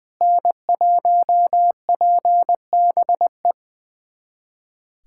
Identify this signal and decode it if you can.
Morse code, heard as a keyed tone, about 21 wpm, 700 Hz. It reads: N1PBE
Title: USA Callsign DE Callsign - 5 Characters - 20wpm